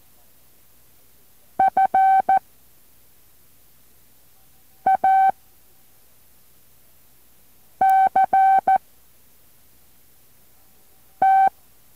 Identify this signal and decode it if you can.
Morse code, heard as a keyed tone, about 14 wpm, 770 Hz. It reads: F A C T